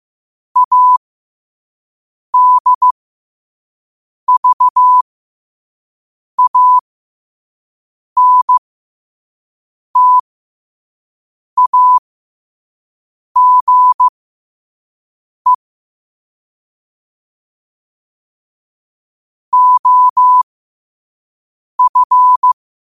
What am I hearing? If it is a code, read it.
Morse code, heard as a keyed tone, about 15 wpm, spaced as for 5 wpm, 1000 Hz. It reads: ADVANTAGE OF
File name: qotdfiles / 5wpm